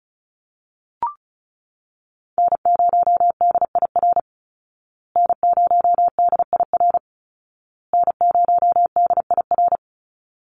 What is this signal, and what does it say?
Morse code, heard as a keyed tone, about 34 wpm, 700 Hz. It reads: N0DIR N0DIR N0DIR
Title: Call Signs Repeated 2x - 35wpm